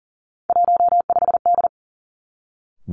Morse code, heard as a keyed tone, about 39 wpm, 700 Hz. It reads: 15D
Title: Three Challenging Letter-Number Combinations 40wpm